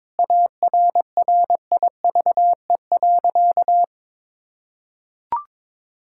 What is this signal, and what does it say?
Morse code, heard as a keyed tone, about 22 wpm, 700 Hz. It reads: ARRIVE.
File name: practice / 22wpm / Sentences with 2-Word Phrases - 22wpm